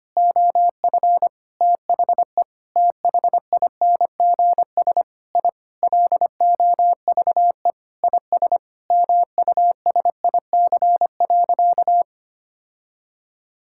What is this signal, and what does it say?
Morse code, heard as a keyed tone, about 25 wpm, 700 Hz. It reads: OF THE THINGS I LOVE IS MUSIC.
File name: practice / 25wpm / Sentences with 3-Word Phrases - 25wpm